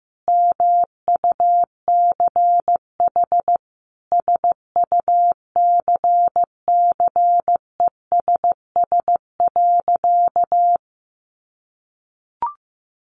Morse code, heard as a keyed tone, about 15 wpm, 700 Hz. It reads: MUCH SUCCESS.
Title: Sentences from Top 900 Words 15wpm